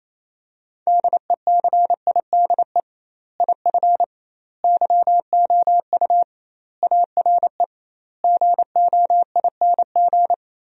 Morse code, heard as a keyed tone, about 28 wpm, 700 Hz. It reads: DECIDE IF YOU ARE GOING
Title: 2x Word-Spacing - Sentences from Top 500 Words - 28wpm